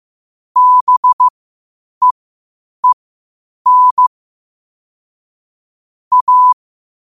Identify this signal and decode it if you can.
Morse code, heard as a keyed tone, about 15 wpm, spaced as for 8 wpm, 1000 Hz. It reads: BEEN A
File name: qotdfiles / 8wpm